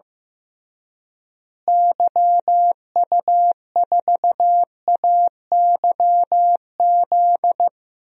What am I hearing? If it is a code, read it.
Morse code, heard as a keyed tone, about 15 wpm, 700 Hz. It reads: YU4AYZ
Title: DXCC-100 Call Signs - 6 Characters - Repeated 3x - 15wpm